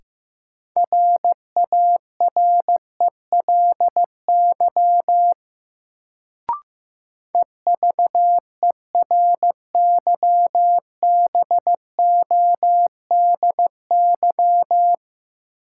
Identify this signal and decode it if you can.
Morse code, heard as a keyed tone, about 16 wpm, 700 Hz. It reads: RARELY EVERYBODY
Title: Sentences from Top 1400 Words - New Word Drill Alt - 15wpm